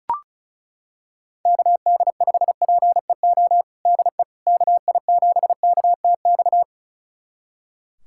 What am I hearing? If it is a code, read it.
Morse code, heard as a keyed tone, about 36 wpm, 700 Hz. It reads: KD5PEO DE KI7KTX
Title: USA Callsign DE Callsign - 6 Characters - 35wpm